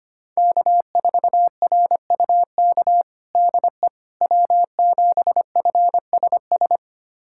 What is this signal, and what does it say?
Morse code, heard as a keyed tone, about 25 wpm, 700 Hz. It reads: K4RUK DE W7FSS
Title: USA Callsign DE Callsign - 5 Characters - 25wpm